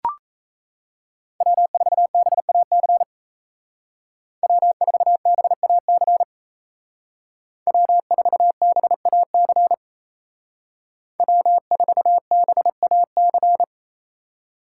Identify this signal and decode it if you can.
Morse code, heard as a keyed tone, about 34 wpm, 700 Hz. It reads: W4BAC W4BAC W4BAC W4BAC
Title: Speed Racing - USA Call Signs 28wpm